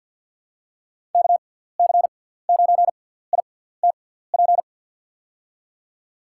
Morse code, heard as a keyed tone, about 49 wpm, spaced as for 17 wpm, 700 Hz. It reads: KC9ITP
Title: USA Call Signs - 6x Character Spacing - 50wpm